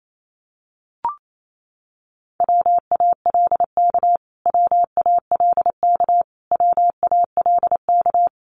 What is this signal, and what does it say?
Morse code, heard as a keyed tone, about 28 wpm, 700 Hz. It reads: WALK WALK WALK
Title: Top 500 Words Repeated 28wpm